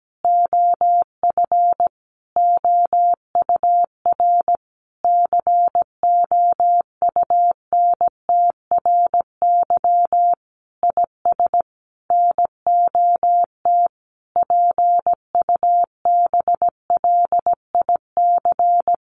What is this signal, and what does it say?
Morse code, heard as a keyed tone, about 17 wpm, 700 Hz. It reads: OF OUR COUNTRY IS NOT PUBLIC
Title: Sentences from Top 1200 Words 17wpm